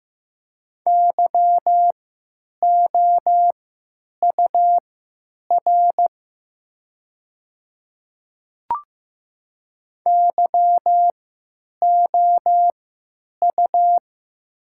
Morse code, heard as a keyed tone, about 15 wpm, spaced as for 9 wpm, 700 Hz. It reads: YOUR YOU
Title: 3x Character Spacing - Top 100 Words - 15wpm